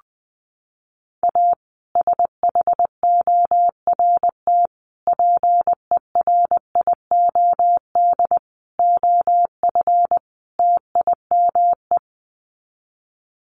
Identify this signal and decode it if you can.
Morse code, heard as a keyed tone, about 20 wpm, 700 Hz. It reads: A SHORT PERIOD OF TIME
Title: Sets of 5 Words Encore 20wpm